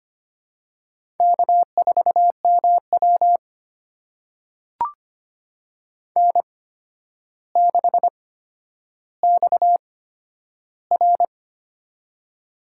Morse code, heard as a keyed tone, about 25 wpm, spaced as for 11 wpm, 700 Hz. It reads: K4MW N6XR
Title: USA Call Signs - 8x Character Spacing - 25wpm